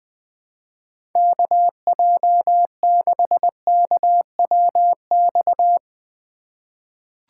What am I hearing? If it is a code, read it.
Morse code, heard as a keyed tone, about 20 wpm, 700 Hz. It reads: KJ6KWX
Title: USA Call Signs - 6 Characters - 6x Character Spacing - 20wpm